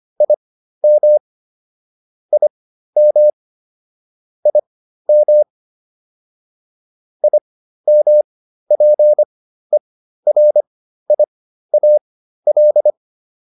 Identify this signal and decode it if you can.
Morse code, heard as a keyed tone, about 25 wpm, spaced as for 13 wpm, 600 Hz. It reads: IM IM IM IMPERIAL